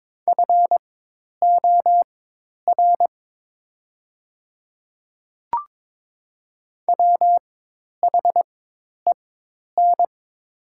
Morse code, heard as a keyed tone, about 23 wpm, spaced as for 10 wpm, 700 Hz. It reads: FOR WHEN
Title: 4x Character Spacing - Top 300 Words - 22wpm